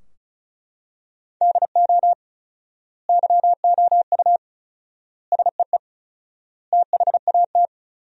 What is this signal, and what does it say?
Morse code, heard as a keyed tone, about 36 wpm, 700 Hz. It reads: DO YOU SEE THAT
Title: 4x Word-Spacing - Sentences from Top 500 Words - 35wpm